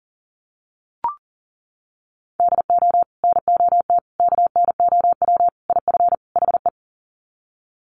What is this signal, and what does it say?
Morse code, heard as a keyed tone, about 40 wpm, 700 Hz. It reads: DO NOT KNOW IF HE
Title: Sets of 5 Words Encore2 40wpm